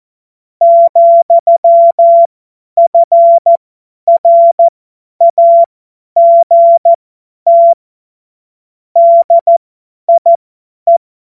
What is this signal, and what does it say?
Morse code, heard as a keyed tone, about 14 wpm, spaced as for 10 wpm, 680 Hz. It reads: , FRAGT DIE
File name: Morsekurs / Witze / Witz 21. September 2021, 50 F70